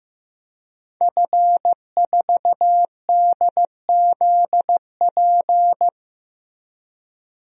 Morse code, heard as a keyed tone, about 15 wpm, 700 Hz. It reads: F4DZP
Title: DXCC-100 Call Signs - 5 Characters - 15wpm